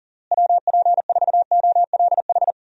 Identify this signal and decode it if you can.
Morse code, heard as a keyed tone, about 40 wpm, 700 Hz. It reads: WP4OLH
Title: USA Call Signs - 6 Characters - Repeated 3x - 40wpm